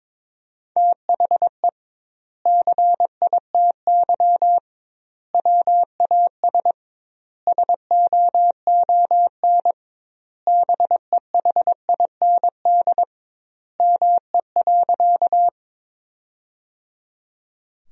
Morse code, heard as a keyed tone, about 22 wpm, 700 Hz. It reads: THE CITY WAS SOON BEHIND ME.
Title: 2x Word-Spacing - Sentences from Top 500 Words - 22wpm